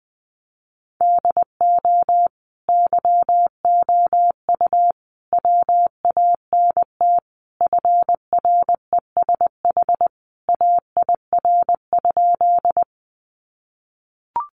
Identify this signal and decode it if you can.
Morse code, heard as a keyed tone, about 20 wpm, 700 Hz. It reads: DO YOU WANT FRESH AIR?
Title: Sentences from Top 1200 Words 20wpm